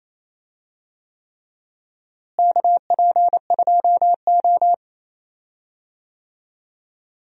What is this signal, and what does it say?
Morse code, heard as a keyed tone, about 28 wpm, 700 Hz. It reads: KP2O
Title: DXCC-100 Call Signs - 4 Characters - Repeated 3x - 28wpm